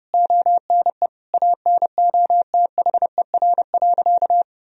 Morse code, heard as a keyed tone, about 32 wpm, 700 Hz. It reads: ONE ANOTHER.
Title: Sentences from Top 1000 Words - Review Part 1 - 30wpm